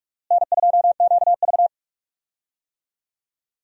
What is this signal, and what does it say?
Morse code, heard as a keyed tone, about 46 wpm, 700 Hz. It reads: N2QV
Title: USA Call Signs - 4 Characters - Repeated 2x - 45wpm